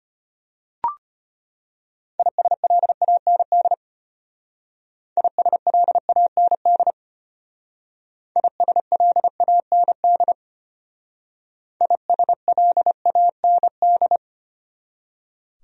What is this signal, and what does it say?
Morse code, heard as a keyed tone, about 32 wpm, 700 Hz. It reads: ISLAND ISLAND ISLAND ISLAND